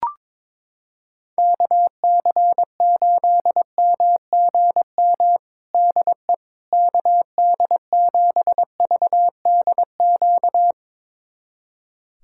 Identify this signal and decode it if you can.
Morse code, heard as a keyed tone, about 22 wpm, 700 Hz. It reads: KC8MGM DE KD7VDQ
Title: USA Callsign DE Callsign - 6 Characters - 22wpm